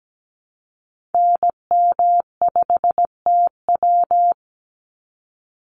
Morse code, heard as a keyed tone, about 17 wpm, 700 Hz. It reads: NM5TW
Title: NA SOTA Activator Callsigns 17wpm